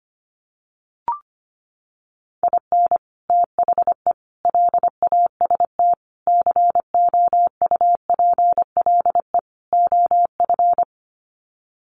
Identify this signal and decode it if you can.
Morse code, heard as a keyed tone, about 25 wpm, 700 Hz. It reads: IN THE LAST COUPLE OF
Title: Sets of 5 Words 25wpm